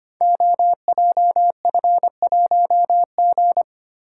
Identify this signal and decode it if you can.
Morse code, heard as a keyed tone, about 25 wpm, 700 Hz. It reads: OJF1G
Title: Code Groups - 5 Letter:Numbers - 25wpm